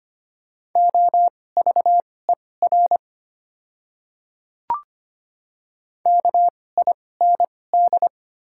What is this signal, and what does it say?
Morse code, heard as a keyed tone, about 25 wpm, spaced as for 18 wpm, 690 Hz. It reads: OVER KIND
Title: 2x Character Spacing - Top 200 Words - 25wpm